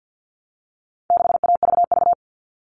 Morse code, heard as a keyed tone, about 50 wpm, 700 Hz. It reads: 6AVV
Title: All Cut Numbers - 4-Digit 50wpm